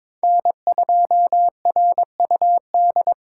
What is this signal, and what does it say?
Morse code, heard as a keyed tone, about 22 wpm, 700 Hz. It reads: N2RUD
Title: USA Call Signs - 3x Character Spacing - 22wpm